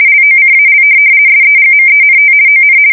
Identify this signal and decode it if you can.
RTTY, about 50 baud. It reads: LOCAL SEGMENT ON
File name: fsk170-50B / teletype-corp